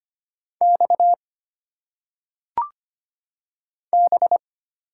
Morse code, heard as a keyed tone, about 25 wpm, 700 Hz. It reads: X B
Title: Single Letters - Phonetic Pronunciation 25wpm